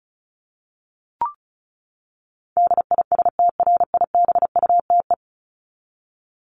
Morse code, heard as a keyed tone, about 34 wpm, 700 Hz. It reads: DISTRIBUTE
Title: Top 1501-1750 Words 35wpm